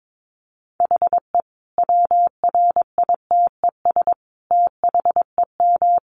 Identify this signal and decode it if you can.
Morse code, heard as a keyed tone, about 22 wpm, 700 Hz. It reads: HE WRITES THEM
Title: Additional Easy Phrases 22wpm